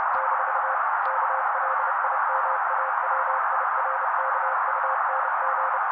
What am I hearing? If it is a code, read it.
Morse code, heard as a keyed tone, about 30 wpm, 530 Hz. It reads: - KLIMAWIRKUNG
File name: daily / 29wpm noise